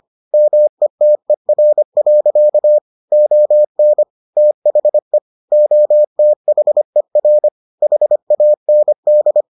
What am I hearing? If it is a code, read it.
Morse code, heard as a keyed tone, about 25 wpm, 600 Hz. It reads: METER. ON THE OTHER HAND